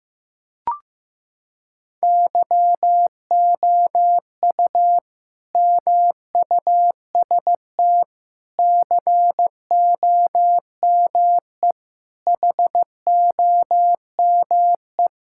Morse code, heard as a keyed tone, about 15 wpm, 700 Hz. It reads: YOU MUST COME HOME
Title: Sentences from Top 1000 Words - Review Part 5 - 15wpm